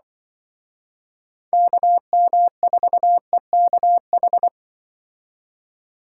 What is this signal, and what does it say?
Morse code, heard as a keyed tone, about 25 wpm, 700 Hz. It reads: KM4EKH
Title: Speed Racing - USA Call Signs 20wpm